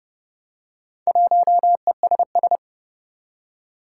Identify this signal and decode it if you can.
Morse code, heard as a keyed tone, about 30 wpm, 700 Hz. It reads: 1ESS
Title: Four Challenging Letter-Number Combinations 30wpm